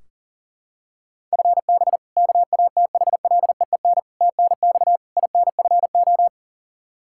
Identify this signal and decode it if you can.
Morse code, heard as a keyed tone, about 39 wpm, 700 Hz. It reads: FB KATHLEEN TNX INFO
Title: Practice QSO - Exchange 3 - More Info - 40wpm